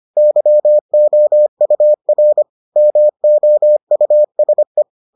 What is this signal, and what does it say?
Morse code, heard as a keyed tone, about 25 wpm, 600 Hz. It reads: YOUR MOUSE